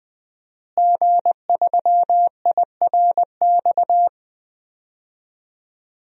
Morse code, heard as a keyed tone, about 20 wpm, 700 Hz. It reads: G3IRX
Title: DXCC-100 Call Signs - 5 Characters - Repeated 3x - 20wpm